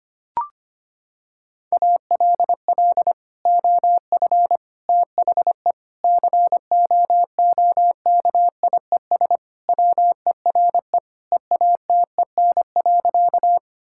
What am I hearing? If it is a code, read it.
Morse code, heard as a keyed tone, about 25 wpm, 700 Hz. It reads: ALL OF THE COOKIES WERE EATEN.
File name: practice / 25wpm / Sentences with 2-Word Phrases - 25wpm